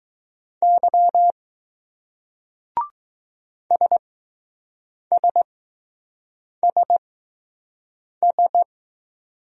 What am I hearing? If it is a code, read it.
Morse code, heard as a keyed tone, about 20 wpm, spaced as for 6 wpm, 700 Hz. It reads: Y SSSS